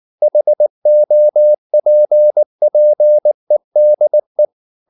Morse code, heard as a keyed tone, about 19 wpm, 600 Hz. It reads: HOPPEDE
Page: Ferie 19 wpm